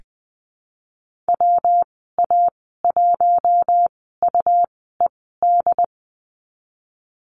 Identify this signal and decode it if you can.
Morse code, heard as a keyed tone, about 20 wpm, spaced as for 14 wpm, 700 Hz. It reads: WA1UED
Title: 154 Lesson 25g - Letter p - Call Signs 20wpm